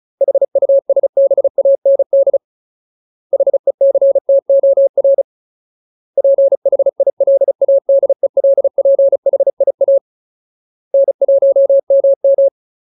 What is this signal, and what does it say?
Morse code, heard as a keyed tone, about 34 wpm, 550 Hz. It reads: HUSBAND HECTOR PHILADELPHIA N1MM
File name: Ultimate Word List / 35WPM Ultimate